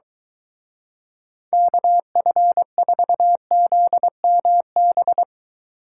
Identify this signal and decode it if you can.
Morse code, heard as a keyed tone, about 23 wpm, 700 Hz. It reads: KF4ZMB